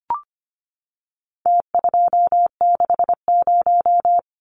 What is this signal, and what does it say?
Morse code, heard as a keyed tone, about 25 wpm, 700 Hz. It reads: T260
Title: Numbers - Contest Style - 4-Digit 25wpm